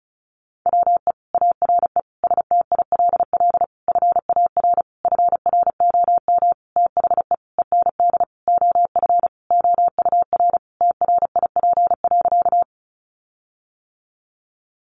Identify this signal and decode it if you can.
Morse code, heard as a keyed tone, about 34 wpm, 700 Hz. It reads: WE ARE STILL FAR FROM THE END OF OUR TRIP.
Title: Sentences from Top 1300 Words 35wpm